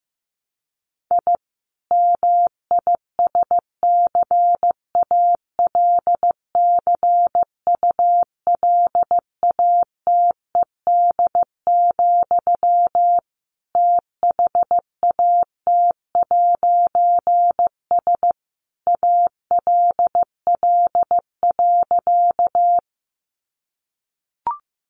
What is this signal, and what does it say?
Morse code, heard as a keyed tone, about 14 wpm, 700 Hz. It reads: I MISCALCULATED, THAT'S ALL.
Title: A Sound Of Thunder 15wpm